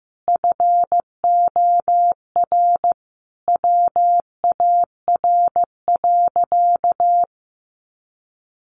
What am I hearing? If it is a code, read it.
Morse code, heard as a keyed tone, about 15 wpm, 700 Hz. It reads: FOR WAR.
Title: Sentences from Top 1300 Words 15wpm